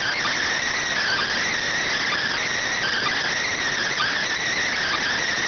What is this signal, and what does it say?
Emergency Alert System (SAME) tones, heard as harsh buzzing bursts. It is heard from 1.0 to 5.0 s.